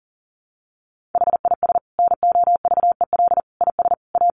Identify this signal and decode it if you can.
Morse code, heard as a keyed tone, about 40 wpm, 700 Hz. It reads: HIS NOVEL IS A
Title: Sentences from Top 1300 Words 40wpm